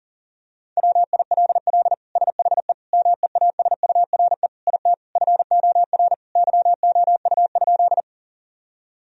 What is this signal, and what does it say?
Morse code, heard as a keyed tone, about 37 wpm, 700 Hz. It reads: WILL SHE MEASURE IT FOR YOU?
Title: Sentences from Top 1000 Words - Review Part 2 - 40wpm